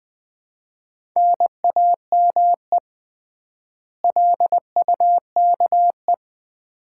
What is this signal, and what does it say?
Morse code, heard as a keyed tone, about 20 wpm, 700 Hz. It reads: NAME LUKE
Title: Common QSO Phrases - 3x Word-Spacing - 20wpm